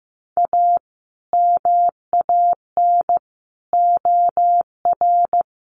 Morse code, heard as a keyed tone, about 15 wpm, 700 Hz. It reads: A MAN OR
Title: Sentences from Top 1100 Words 15wpm